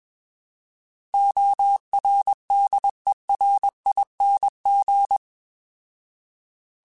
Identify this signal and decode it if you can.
Morse code, heard as a keyed tone, about 21 wpm, 800 Hz. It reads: ORDERING
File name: Alphabetical / Alphabetical mp3s / Os @20